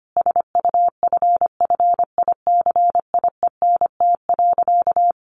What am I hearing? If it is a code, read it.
Morse code, heard as a keyed tone, about 25 wpm, 700 Hz. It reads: SUFFICIENT.
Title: Sentences from Top 1400 Words 25wpm